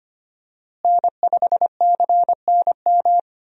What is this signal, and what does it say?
Morse code, heard as a keyed tone, about 25 wpm, 700 Hz. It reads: N5CNM
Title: USA Call Signs - 5 Characters - Repeated 2x - 25wpm